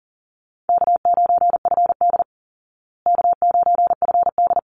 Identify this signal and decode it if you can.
Morse code, heard as a keyed tone, about 40 wpm, 700 Hz. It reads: K9FD K9FD
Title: Call Signs Repeated 3x - 40wpm